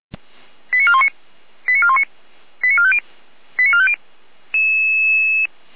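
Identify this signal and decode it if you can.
Five-tone selective call (ZVEI), heard as quick runs of five tones, tones of about 70 ms. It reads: 89419 89419 89450 89450
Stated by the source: ZVEI five-tone call 88650 received by SDR.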